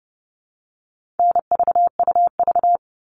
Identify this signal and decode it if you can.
Morse code, heard as a keyed tone, about 30 wpm, 700 Hz. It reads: NVUV
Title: All Cut Numbers - 4-Digit 30wpm